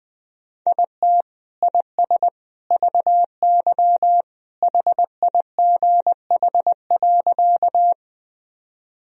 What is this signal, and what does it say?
Morse code, heard as a keyed tone, about 20 wpm, 700 Hz. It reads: IT IS VY HIGH.
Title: Sentences from Top 200 Words and Abbreviations 20wpm